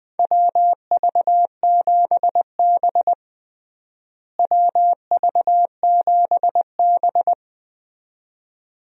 Morse code, heard as a keyed tone, about 21 wpm, 700 Hz. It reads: WV7B WV7B
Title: USA Call Signs - 4 Characters - Repeated 2x - 20wpm